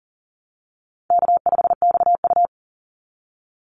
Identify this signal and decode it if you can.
Morse code, heard as a keyed tone, about 40 wpm, 700 Hz. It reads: K5XU
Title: Easy Call Signs Repeated 2x - 40wpm